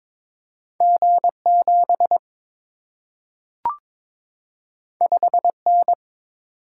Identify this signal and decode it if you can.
Morse code, heard as a keyed tone, about 22 wpm, 700 Hz. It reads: G 7 5 N
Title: Single-Letter Single-Number 22wpm